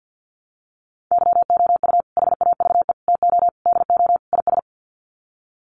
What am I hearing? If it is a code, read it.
Morse code, heard as a keyed tone, about 50 wpm, 700 Hz. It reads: YOU HAVE TO DO IS
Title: Sets of 5 Words 50wpm